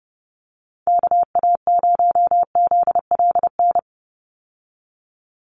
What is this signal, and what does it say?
Morse code, heard as a keyed tone, about 30 wpm, 700 Hz. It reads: KA0ZLN
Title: USA Call Signs 30wpm